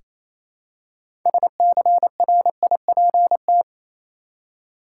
Morse code, heard as a keyed tone, about 28 wpm, 700 Hz. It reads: SCRIPT
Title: Six Letter Words - 28wpm